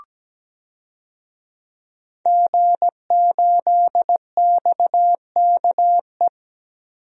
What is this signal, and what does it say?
Morse code, heard as a keyed tone, about 17 wpm, 700 Hz. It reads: G8XKE
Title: DXCC-100 Call Signs - 5 Characters - Repeated 2x - 17wpm